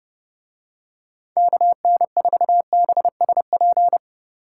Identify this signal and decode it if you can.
Morse code, heard as a keyed tone, about 30 wpm, 700 Hz. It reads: KN4BSP